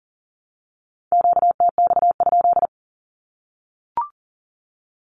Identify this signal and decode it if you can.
Morse code, heard as a keyed tone, about 40 wpm, 700 Hz. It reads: QTX?
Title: Q-Codes Repeated 3x 40wpm